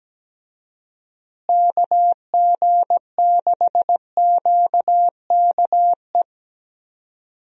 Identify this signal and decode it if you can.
Morse code, heard as a keyed tone, about 17 wpm, 700 Hz. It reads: KG6QKE
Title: USA Call Signs - 6 Characters - Repeated 2x - 17wpm